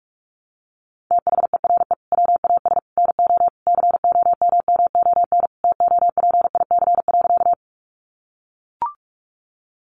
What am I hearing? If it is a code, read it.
Morse code, heard as a keyed tone, about 44 wpm, 700 Hz. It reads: THERE WAS NO COMMON TOPIC.
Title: Sentences from Top 1200 Words 45wpm